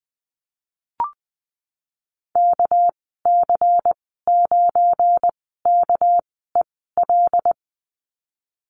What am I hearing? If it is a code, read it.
Morse code, heard as a keyed tone, about 20 wpm, spaced as for 14 wpm, 700 Hz. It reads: KC9KEL
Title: 213 Lesson 34g - Number 8 - Call Signs 20wpm